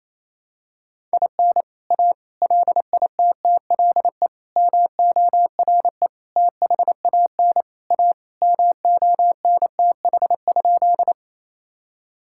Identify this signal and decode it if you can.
Morse code, heard as a keyed tone, about 28 wpm, 700 Hz. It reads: IN A LITTLE MORE THAN A MONTH?
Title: Sentences from Top 1000 Words - Review Part 5 - 28wpm